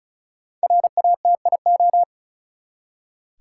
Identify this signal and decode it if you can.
Morse code, heard as a keyed tone, about 36 wpm, 700 Hz. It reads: RATIO
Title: Five Letter Words - 35wpm